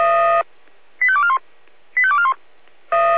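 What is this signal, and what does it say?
Five-tone selective call (ZVEI), heard as quick runs of five tones, tones of about 80 ms. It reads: 85241 85241
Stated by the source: ZVEI five-tone selective call received on SDR.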